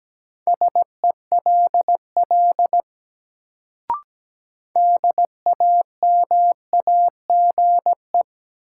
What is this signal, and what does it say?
Morse code, heard as a keyed tone, about 17 wpm, 700 Hz. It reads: SELL DAMAGE
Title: Sentences from Top 1100 Words - New Word Drill Alt - 17wpm